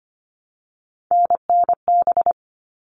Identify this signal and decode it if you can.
Morse code, heard as a keyed tone, about 25 wpm, 700 Hz. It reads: NNB
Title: All Cut Numbers - 3-Digit 25wpm